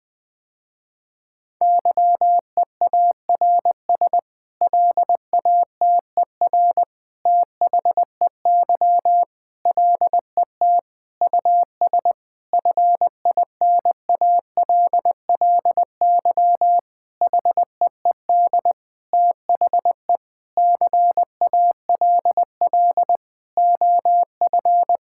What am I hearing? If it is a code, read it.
Morse code, heard as a keyed tone, about 20 wpm, 700 Hz. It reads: YEARS LATER THEY LET US FINALLY HEED THE CALL OF
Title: Sentences with 8-Word Phrases - Spoken Beginning - 20wpm